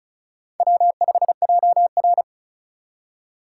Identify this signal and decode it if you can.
Morse code, heard as a keyed tone, about 34 wpm, 700 Hz. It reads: W5JR
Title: USA Call Signs - 4 Characters - 4x Character Spacing - 35wpm